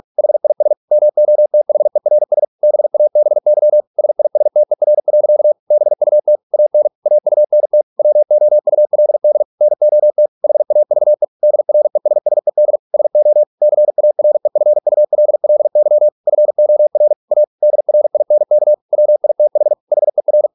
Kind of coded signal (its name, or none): Morse code